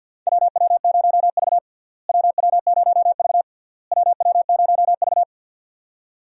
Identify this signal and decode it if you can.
Morse code, heard as a keyed tone, about 47 wpm, 700 Hz. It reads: WW0V WW0V WW0V
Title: USA Call Signs - 4 Characters - Repeated 3x - 50wpm